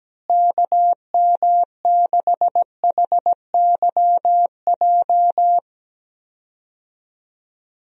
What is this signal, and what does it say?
Morse code, heard as a keyed tone, about 17 wpm, 700 Hz. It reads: KM6HYJ
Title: USA Call Signs - 6 Characters - Repeated 3x - 17wpm